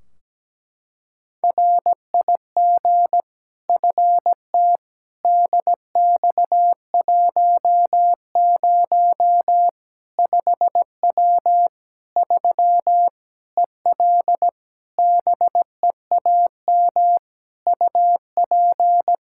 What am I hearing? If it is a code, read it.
Morse code, heard as a keyed tone, about 17 wpm, 700 Hz. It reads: RIG FT DX10 5W 3 EL BEAM UP
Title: Practice QSO - Exchange 2 - Info - 17wpm